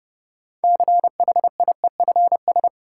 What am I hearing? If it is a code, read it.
Morse code, heard as a keyed tone, about 32 wpm, 700 Hz. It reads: CHIEFS
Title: Sets of 5 Words Encore - New Word Drill - 30wpm